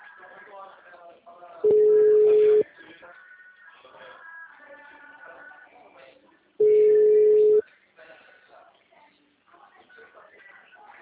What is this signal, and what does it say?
Ringback tone (Europe, 425 Hz), with speech.